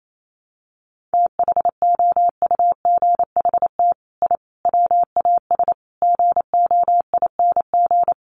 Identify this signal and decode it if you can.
Morse code, heard as a keyed tone, about 28 wpm, 700 Hz. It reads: THOUGHT I WAS GOING
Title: Sets of 5 Words Encore2 28wpm